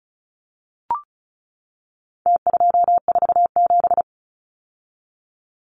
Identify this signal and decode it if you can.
Morse code, heard as a keyed tone, about 34 wpm, 700 Hz. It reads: T247
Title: Numbers - Contest Style - 4-Digit 35wpm